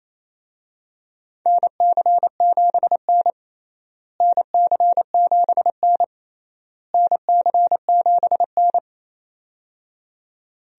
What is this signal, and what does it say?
Morse code, heard as a keyed tone, about 28 wpm, 700 Hz. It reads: NC7N NC7N NC7N
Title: USA Call Signs - 4 Characters - Repeated 3x - 28wpm